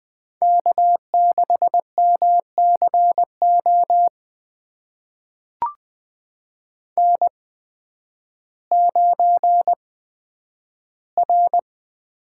Morse code, heard as a keyed tone, about 20 wpm, spaced as for 11 wpm, 700 Hz. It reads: K6MCO N9R
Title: USA Call Signs - 5 Characters - 8x Character Spacing - 20wpm